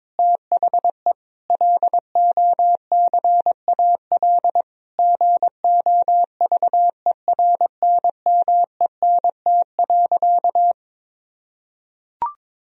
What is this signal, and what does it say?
Morse code, heard as a keyed tone, about 22 wpm, 700 Hz. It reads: THE LOCAL GOVERNMENT.
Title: Sentences from Top 1000 Words - Review Part 3 - 22wpm